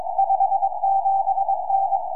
Morse code, heard as a keyed tone, about 22 wpm, 780 Hz. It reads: 57N
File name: oy3je 4m